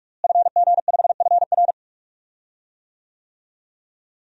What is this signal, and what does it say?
Morse code, heard as a keyed tone, about 46 wpm, 700 Hz. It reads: FG5FR
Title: DXCC-100 Call Signs - 5 Characters - Repeated 2x - 45wpm